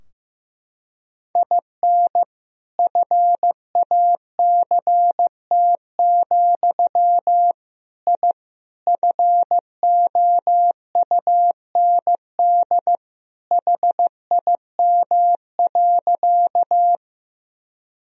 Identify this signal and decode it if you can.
Morse code, heard as a keyed tone, about 16 wpm, 700 Hz. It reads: IN FACT, I FOUND HIM.
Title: Sentences from Top 1000 Words - Review Part 3 - 15wpm